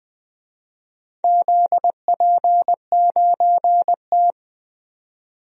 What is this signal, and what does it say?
Morse code, heard as a keyed tone, about 20 wpm, 700 Hz. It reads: ZP9T
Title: DXCC-100 Call Signs - 4 Characters - 20wpm